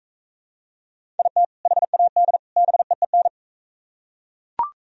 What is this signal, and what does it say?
Morse code, heard as a keyed tone, about 41 wpm, 700 Hz. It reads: IT HAD BEEN